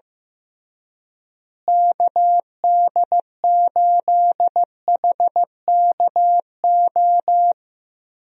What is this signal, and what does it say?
Morse code, heard as a keyed tone, about 15 wpm, 700 Hz. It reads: KD8HKO
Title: USA Call Signs - 6 Characters - Repeated 3x - 15wpm